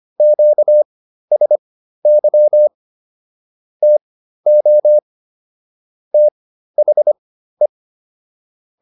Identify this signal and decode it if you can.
Morse code, heard as a keyed tone, about 25 wpm, spaced as for 13 wpm, 600 Hz. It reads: QSY TO THE